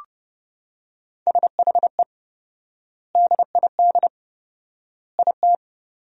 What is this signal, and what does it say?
Morse code, heard as a keyed tone, about 30 wpm, 700 Hz. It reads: SHE DID IT
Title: 4x Word-Spacing - Sentences from Top 400 Words - 30wpm